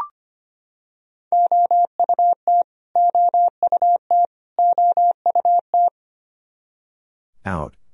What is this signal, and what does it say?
Morse code, heard as a keyed tone, about 25 wpm, 700 Hz. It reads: OUT OUT OUT
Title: Top 100 Words Repeated 25wpm